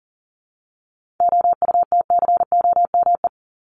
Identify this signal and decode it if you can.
Morse code, heard as a keyed tone, about 40 wpm, 700 Hz. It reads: OUTCOME
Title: Top 1501-1750 Words 40wpm